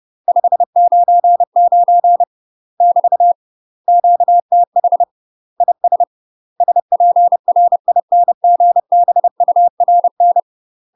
Morse code, heard as a keyed tone, about 30 wpm, 700 Hz. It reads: 599 <BT> QTH IS SPRINGBURN
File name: UK MASTER QSOs TRAINING / 30WPM / UK MASTER QSOs 0028